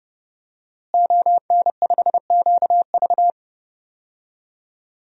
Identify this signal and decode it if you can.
Morse code, heard as a keyed tone, about 30 wpm, 700 Hz. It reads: ON5QV
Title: DXCC-100 Call Signs - 5 Characters - Repeated 2x - 30wpm